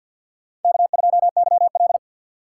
Morse code, heard as a keyed tone, about 53 wpm, 700 Hz. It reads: K2YL
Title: USA Call Signs - 4 Characters - 3x Character Spacing - 50wpm